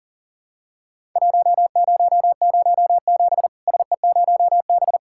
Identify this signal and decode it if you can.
Morse code, heard as a keyed tone, about 37 wpm, 700 Hz. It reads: 1007 SE0B
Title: Worked All Europe DX Contest - QTC Exchange - 40wpm